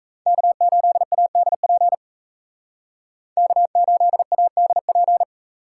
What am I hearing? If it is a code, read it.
Morse code, heard as a keyed tone, about 36 wpm, 700 Hz. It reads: K8ADP K8ADP